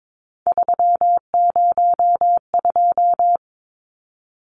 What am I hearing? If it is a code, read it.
Morse code, heard as a keyed tone, about 22 wpm, 700 Hz. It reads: 302
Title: Numbers - 3-Digit 22wpm